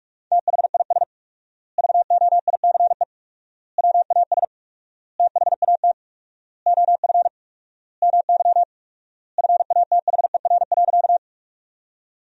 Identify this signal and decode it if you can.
Morse code, heard as a keyed tone, about 46 wpm, 700 Hz. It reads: THIS VOICE WAS THAT OF MY FATHER.
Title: 4x Word-Spacing - Sentences from Top 500 Words - 45wpm